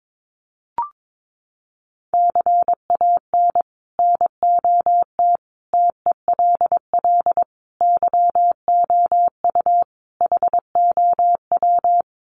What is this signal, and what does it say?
Morse code, heard as a keyed tone, about 22 wpm, 700 Hz. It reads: CAN NOT TELL YOU HOW
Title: Sets of 5 Words Encore2 22wpm